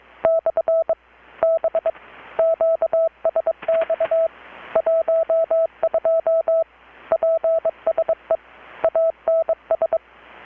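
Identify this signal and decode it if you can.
Morse code, heard as a keyed tone, about 22 wpm, 650 Hz, with radio noise. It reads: / B QSX 12 PSE ANS
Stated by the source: Morse from KPH's Night of Nights XX maritime radiotelegraphy event, received in Chiapas, Mexico about 3,800 km away on a Kenwood R-600 with a grounded 10 m long-wire antenna.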